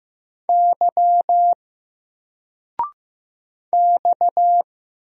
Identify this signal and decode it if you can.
Morse code, heard as a keyed tone, about 15 wpm, 700 Hz. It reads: Y X
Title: Single Character (Letter:Number:Punctuation:Prosign) 15wpm